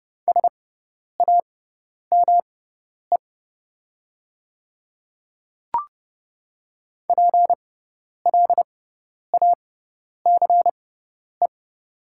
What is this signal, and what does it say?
Morse code, heard as a keyed tone, about 30 wpm, spaced as for 10 wpm, 700 Hz. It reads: SAME PLACE
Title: 6x Character Spacing - Top 300 Words - 30wpm